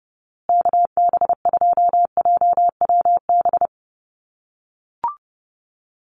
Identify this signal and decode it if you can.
Morse code, heard as a keyed tone, about 30 wpm, 700 Hz. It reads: KB2JWB
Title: USA Call Signs Repeated 3x 30wpm